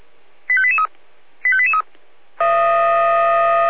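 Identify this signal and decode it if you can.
Five-tone selective call (ZVEI), heard as quick runs of five tones, tones of about 70 ms. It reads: 85883 85883
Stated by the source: ZVEI five-tone selective call received on SDR.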